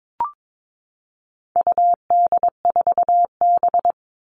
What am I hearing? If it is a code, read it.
Morse code, heard as a keyed tone, about 22 wpm, 700 Hz. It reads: UD4B
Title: All Cut Numbers - 4-Digit 22wpm